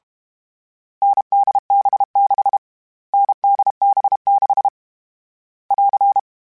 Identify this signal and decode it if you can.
Morse code, heard as a keyed tone, about 32 wpm, 800 Hz. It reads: NDB6 NDB6 <AR>
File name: Variable Speed / Variable Speed-mp3s / GZ GZ7 sp @30